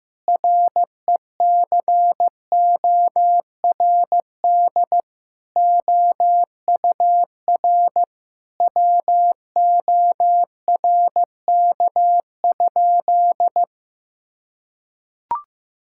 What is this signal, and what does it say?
Morse code, heard as a keyed tone, about 15 wpm, 700 Hz. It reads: RECORD OUR WORK?
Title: Sentences from Top 1000 Words - Review Part 6 - 15wpm